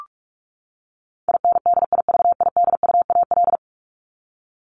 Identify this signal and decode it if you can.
Morse code, heard as a keyed tone, about 44 wpm, 700 Hz. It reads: INDIVIDUAL
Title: ES:S Suffix - Top 100 Words with Base Words 45wpm